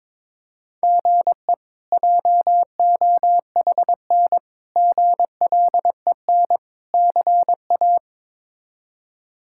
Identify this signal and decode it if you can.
Morse code, heard as a keyed tone, about 22 wpm, 700 Hz. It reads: GE JOHN GLEN CA
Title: K1USN SST QSO Practice - 22wpm